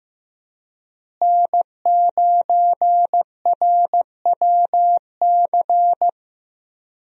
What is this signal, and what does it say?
Morse code, heard as a keyed tone, about 15 wpm, 700 Hz. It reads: N9RWC
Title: USA Call Signs - 5 Characters - Repeated 3x - 15wpm